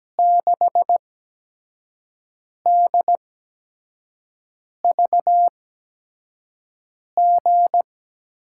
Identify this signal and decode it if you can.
Morse code, heard as a keyed tone, about 17 wpm, spaced as for 5 wpm, 700 Hz. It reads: 6DVG
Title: USA Call Signs - 6 Characters - 8x Character Spacing - 17wpm